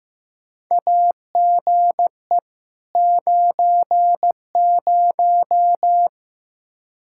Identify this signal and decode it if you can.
Morse code, heard as a keyed tone, about 15 wpm, 700 Hz. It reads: AGE 90
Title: K1USN SST QSO Practice 15wpm